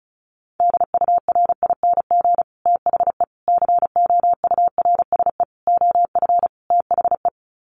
Morse code, heard as a keyed tone, about 34 wpm, 700 Hz. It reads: DURING THE COURSE OF THE
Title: Sets of 5 Words Encore 35wpm